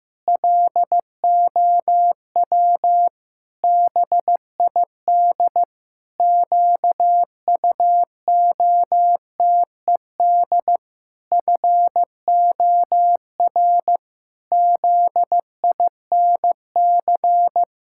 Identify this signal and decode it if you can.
Morse code, heard as a keyed tone, about 15 wpm, 700 Hz. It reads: LOW BID QUOTED FOR ZINC
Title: Pangram Challenge v2 - 15wpm